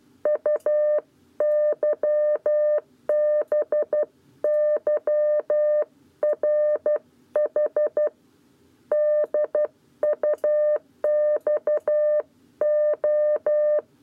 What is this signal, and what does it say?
Morse code, heard as a keyed tone, about 12 wpm, 600 Hz. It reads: UYBYRH DUXO
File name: 7b lezione cw 1